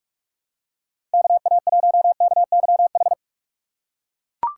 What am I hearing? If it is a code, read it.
Morse code, heard as a keyed tone, about 41 wpm, 700 Hz. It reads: KA1KYH
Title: USA Call Signs - 6 Characters - 2x Character Spacing - 45wpm